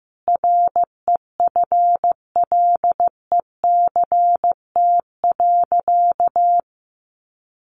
Morse code, heard as a keyed tone, about 14 wpm, 700 Hz. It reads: REFLECT.
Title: Sentences from Top 1500 Words 15wpm